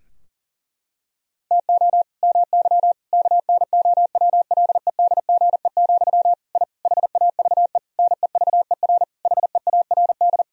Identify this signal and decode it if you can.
Morse code, heard as a keyed tone, about 40 wpm, 700 Hz. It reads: TO MY KNOWLEDGE, I HAVE NEVER HEARD
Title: Sentences from Top 1000 Words - Review Part 3 - 40wpm